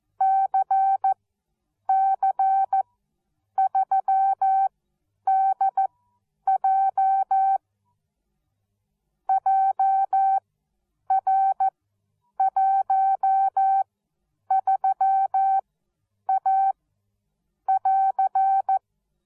Morse code, heard as a keyed tone, about 14 wpm, spaced as for 8 wpm, 770 Hz. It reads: CC3DJ JR13A<AR>